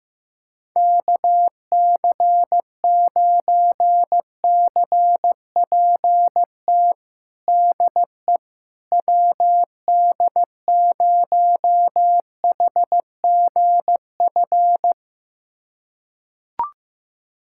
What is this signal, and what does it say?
Morse code, heard as a keyed tone, about 15 wpm, 700 Hz. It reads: KC9CPT DE WD0HGF
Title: USA Callsign DE Callsign - 6 Characters - 15wpm